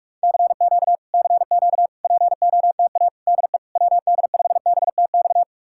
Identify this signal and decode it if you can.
Morse code, heard as a keyed tone, about 67 wpm, spaced as for 56 wpm, 700 Hz. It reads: CQ CQ POTA DE WD5BTX
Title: Speed-Racing Common QSO Phrases Encore - 45wpm